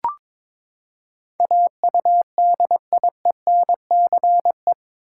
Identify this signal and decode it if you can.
Morse code, heard as a keyed tone, about 21 wpm, 700 Hz. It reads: AUDIENCE
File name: practice / 22wpm / Eight Letter Words - 22wpm